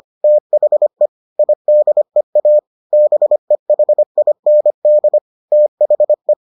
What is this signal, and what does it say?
Morse code, heard as a keyed tone, about 25 wpm, 600 Hz. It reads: THE IDEA BEHIND THE